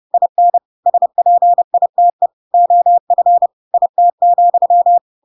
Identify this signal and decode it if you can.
Morse code, heard as a keyed tone, about 30 wpm, 700 Hz. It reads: IN SPITE OF IT,
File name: Pinocchio / Pinocchio ch6 30 30 ik1hge